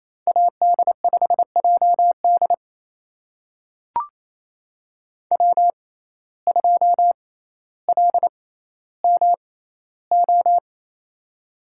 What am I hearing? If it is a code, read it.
Morse code, heard as a keyed tone, about 28 wpm, spaced as for 14 wpm, 700 Hz. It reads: AD5JD W2LMO
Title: USA Call Signs - 5 Characters - 6x Character Spacing - 28wpm